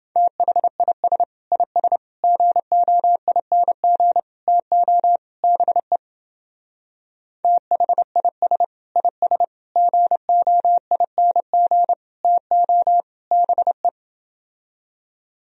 Morse code, heard as a keyed tone, about 28 wpm, 700 Hz. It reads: THIS IS GOING TO BE THIS IS GOING TO BE